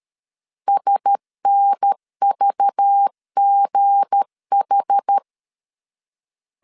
Morse code, heard as a keyed tone, about 13 wpm, 800 Hz. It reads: SNVGH
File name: CW / DET / 1 9 1500C S C 12wpm 3811